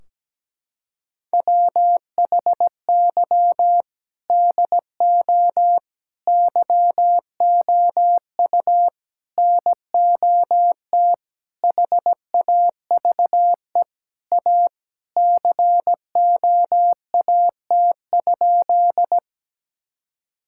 Morse code, heard as a keyed tone, about 17 wpm, 700 Hz. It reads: WHY DO YOU NOT HAVE A COAT?
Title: Sentences from Top 1600 Words 17wpm